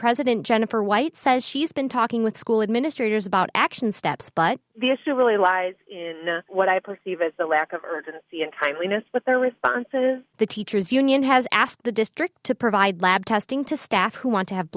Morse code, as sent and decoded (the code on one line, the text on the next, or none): none